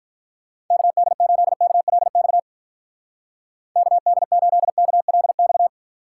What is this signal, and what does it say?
Morse code, heard as a keyed tone, about 45 wpm, 700 Hz. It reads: KD8KLX KD8KLX